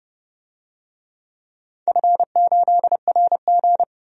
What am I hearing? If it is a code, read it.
Morse code, heard as a keyed tone, about 30 wpm, 700 Hz. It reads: F8RG
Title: DXCC-100 Call Signs - 4 Characters - 30wpm